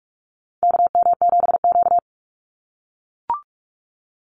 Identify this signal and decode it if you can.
Morse code, heard as a keyed tone, about 44 wpm, 700 Hz. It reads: KM7Q
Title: Call Signs - 45wpm